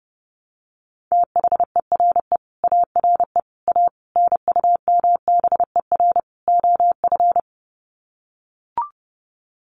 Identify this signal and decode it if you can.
Morse code, heard as a keyed tone, about 30 wpm, 700 Hz. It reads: THERE ARE A NUMBER OF
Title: Sets of 5 Words 30wpm